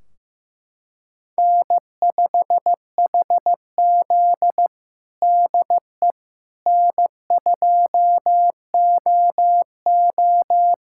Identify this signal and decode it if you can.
Morse code, heard as a keyed tone, about 15 wpm, 700 Hz. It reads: N5HZ DE N2OO
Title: Practice QSO - Exchange 4 - Ending - 15wpm